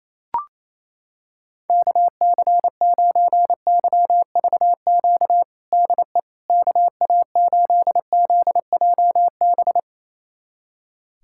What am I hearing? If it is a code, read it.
Morse code, heard as a keyed tone, about 28 wpm, 700 Hz. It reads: KC9YVQ DE KA8ZJB
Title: USA Callsign DE Callsign - 6 Characters - 28wpm